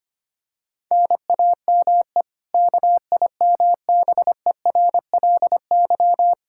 Morse code, heard as a keyed tone, about 25 wpm, 700 Hz. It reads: NAME KIMBERLY
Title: Practice QSO - Exchange 1 - Beginning - 25wpm